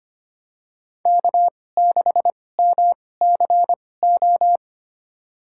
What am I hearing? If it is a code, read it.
Morse code, heard as a keyed tone, about 25 wpm, spaced as for 18 wpm, 700 Hz. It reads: K6MCO
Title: USA Call Signs - 5 Characters - 2x Character Spacing - 25wpm